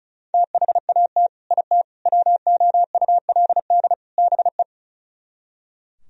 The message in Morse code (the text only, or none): THAT IT WOULD BE